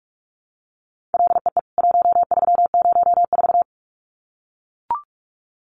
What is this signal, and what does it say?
Morse code, heard as a keyed tone, about 44 wpm, 700 Hz. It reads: LEE 1304
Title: CWT Exchange 45wpm